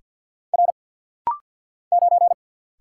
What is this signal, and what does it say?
Morse code, heard as a keyed tone, about 51 wpm, 690 Hz. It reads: R 9
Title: Single Letter-Number - Rapid-Fire 50wpm